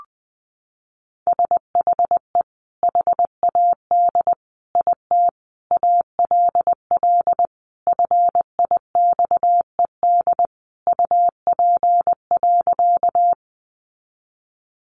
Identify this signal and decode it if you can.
Morse code, heard as a keyed tone, about 20 wpm, 700 Hz. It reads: SHE HAD IT ALL FIXED UP.
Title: Sentences from Top 1100 Words 20wpm